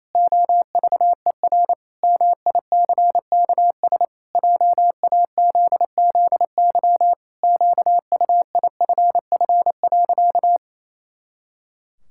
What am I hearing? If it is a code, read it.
Morse code, heard as a keyed tone, about 28 wpm, 700 Hz. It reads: OVER MICKS JAZZY QUIFF.
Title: Pangram Challenge v2 - 28wpm